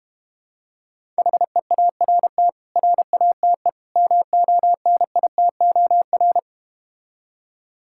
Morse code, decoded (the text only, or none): HEART RATE MONITOR